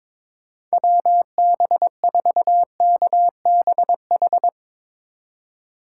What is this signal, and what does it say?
Morse code, heard as a keyed tone, about 22 wpm, 700 Hz. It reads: WB4KBH
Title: USA Call Signs - 3x Character Spacing - 22wpm